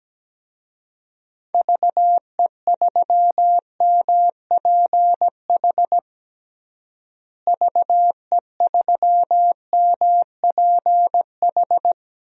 Morse code, heard as a keyed tone, about 17 wpm, 700 Hz. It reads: VE3MPH VE3MPH
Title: DXCC-100 Call Signs - 6 Characters - Repeated 3x - 17wpm